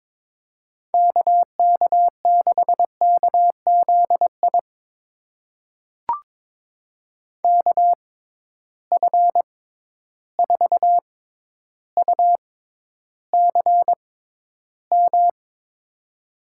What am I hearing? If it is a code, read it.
Morse code, heard as a keyed tone, about 22 wpm, spaced as for 11 wpm, 700 Hz. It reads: KK6KZI KF4UCM
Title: USA Call Signs - 6x Character Spacing - 22wpm